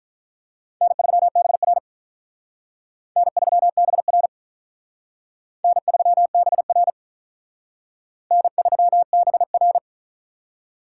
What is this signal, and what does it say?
Morse code, heard as a keyed tone, about 45 wpm, 700 Hz. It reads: N3BR N3BR N3BR N3BR